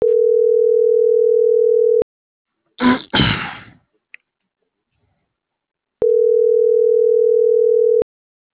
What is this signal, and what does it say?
Ringback tone (North America), 440 + 480 Hz.